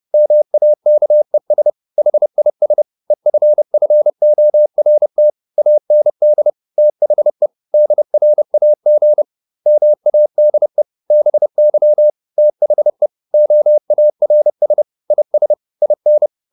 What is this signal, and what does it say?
Morse code, heard as a keyed tone, about 30 wpm, 600 Hz. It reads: MAKES HIS EFFORT AND THE DRAG MADE BY THE OARS IS IN